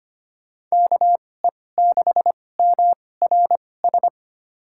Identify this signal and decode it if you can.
Morse code, heard as a keyed tone, about 25 wpm, spaced as for 18 wpm, 700 Hz. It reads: KE6MRS
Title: USA Call Signs - 6 Characters - 2x Character Spacing - 25wpm